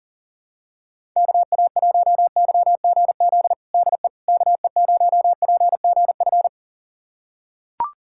Morse code, heard as a keyed tone, about 40 wpm, 700 Hz. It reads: KA1YGZ DE KE0PGF
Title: USA Callsign DE Callsign - 6 Characters - 40wpm